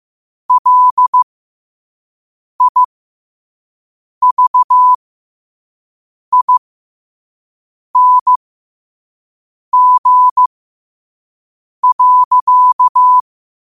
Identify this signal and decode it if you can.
Morse code, heard as a keyed tone, about 15 wpm, spaced as for 5 wpm, 1000 Hz. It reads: LIVING.
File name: qotdfiles / 5wpm